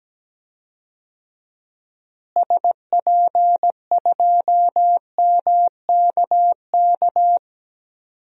Morse code, heard as a keyed tone, about 17 wpm, 700 Hz. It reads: SP2MKK
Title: DXCC-100 Call Signs - 6 Characters - Repeated 3x - 17wpm